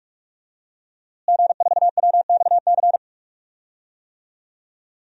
Morse code, heard as a keyed tone, about 48 wpm, 700 Hz. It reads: G4WXC
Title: DXCC-100 Call Signs - 5 Characters - Repeated 2x - 45wpm